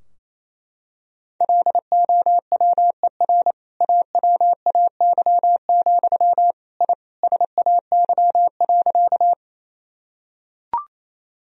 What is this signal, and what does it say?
Morse code, heard as a keyed tone, about 28 wpm, 700 Hz. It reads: LOWER AWAY, I SAY.
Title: Sentences from Top 1000 Words - Review Part 4 - 28wpm